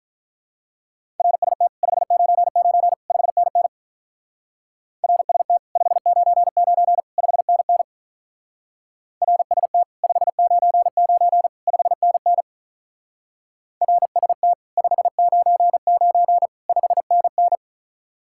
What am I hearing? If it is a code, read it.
Morse code, heard as a keyed tone, about 42 wpm, 700 Hz. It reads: RST 599 5NN RST 599 5NN RST 599 5NN RST 599 5NN
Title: Speed-Racing Common QSO Phrases - 35wpm